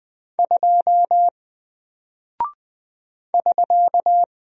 Morse code, heard as a keyed tone, about 20 wpm, 700 Hz. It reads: 2 <SK>
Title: Single Character (Letter:Number:Punctuation:Prosign) Alt 20wpm